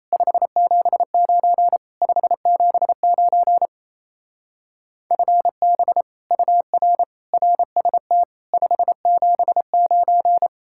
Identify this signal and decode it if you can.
Morse code, heard as a keyed tone, about 35 wpm, 700 Hz. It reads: 579 579 FB UR RST 579
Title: Speed-Racing Common QSO Phrases Encore - 28wpm